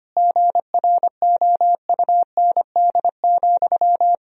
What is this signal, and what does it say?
Morse code, heard as a keyed tone, about 25 wpm, 700 Hz. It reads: GROUND,
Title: Sentences from Top 1000 Words - Review Part 2 - 25wpm